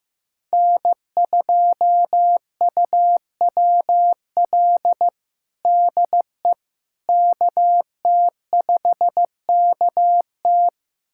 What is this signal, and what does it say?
Morse code, heard as a keyed tone, about 16 wpm, 700 Hz. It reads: N2UWL DE KT5KT
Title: USA Callsign DE Callsign - 5 Characters - 15wpm